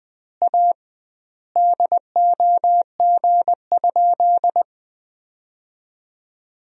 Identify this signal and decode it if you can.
Morse code, heard as a keyed tone, about 21 wpm, 700 Hz. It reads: A DOG?
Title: 2x Word-Spacing - Sentences from Top 400 Words - 20wpm